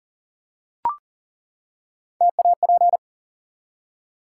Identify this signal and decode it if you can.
Morse code, heard as a keyed tone, about 40 wpm, 700 Hz. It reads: TAP
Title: Three Letter Words - 40wpm